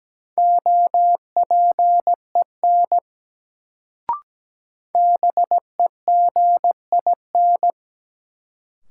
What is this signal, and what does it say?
Morse code, heard as a keyed tone, about 17 wpm, 700 Hz. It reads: OPEN BEGIN
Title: Sentences from Top 300 Words - New Word Drill - 17wpm